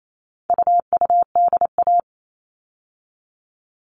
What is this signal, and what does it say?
Morse code, heard as a keyed tone, about 28 wpm, 700 Hz. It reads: UUDA
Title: All Cut Numbers - 4-Digit 28wpm